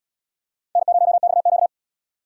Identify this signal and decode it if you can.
Morse code, heard as a keyed tone, about 78 wpm, 700 Hz. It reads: N0XQ